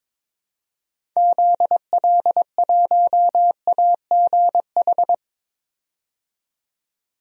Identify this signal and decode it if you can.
Morse code, heard as a keyed tone, about 22 wpm, 700 Hz. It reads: ZL1AGH
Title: DXCC-100 Call Signs - 6 Characters - 22wpm